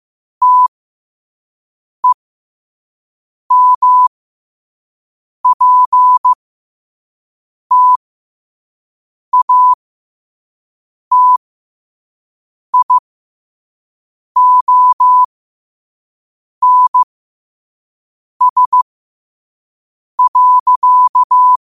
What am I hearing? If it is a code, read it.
Morse code, heard as a keyed tone, about 15 wpm, spaced as for 5 wpm, 1000 Hz. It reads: TEMPTATIONS.
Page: qotd-at-5-wpm